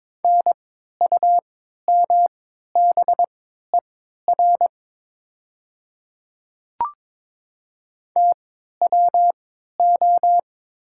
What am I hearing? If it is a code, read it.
Morse code, heard as a keyed tone, about 22 wpm, spaced as for 13 wpm, 700 Hz. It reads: NUMBER TWO
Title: 3x Character Spacing - Top 100 Words - 22wpm